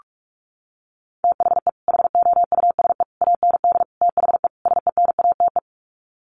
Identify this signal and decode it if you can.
Morse code, heard as a keyed tone, about 44 wpm, 700 Hz. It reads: THE HOUSE AND THE SENATE
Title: Sets of 5 Words Encore2 45wpm